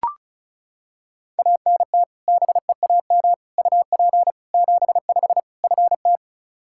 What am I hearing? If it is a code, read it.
Morse code, heard as a keyed tone, about 34 wpm, 700 Hz. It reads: ANT BEAM UP 75 FT
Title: Practice QSO - Exchange 2 - Info - 35wpm